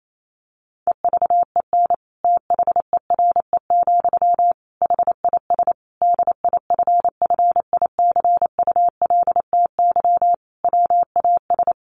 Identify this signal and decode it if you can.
Morse code, heard as a keyed tone, about 28 wpm, 700 Hz. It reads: EVEN THERE, HIS DIFFICULTY WAS
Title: Sentences from Top 1500 Words 28wpm